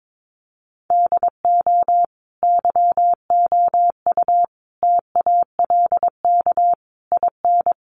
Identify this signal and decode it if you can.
Morse code, heard as a keyed tone, about 22 wpm, 700 Hz. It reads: DO YOU TALK IN
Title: Sentences from Top 500 Words 22wpm